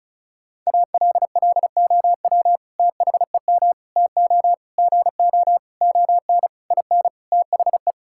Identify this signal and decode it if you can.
Morse code, heard as a keyed tone, about 36 wpm, 700 Hz. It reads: ALLOW THEM TO GO ON IN THE
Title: Sentences from Top 1000 Words - Review Part 5 - 35wpm